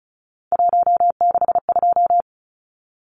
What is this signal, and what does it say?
Morse code, heard as a keyed tone, about 34 wpm, 700 Hz. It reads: 162
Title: Numbers - 3-Digit 35wpm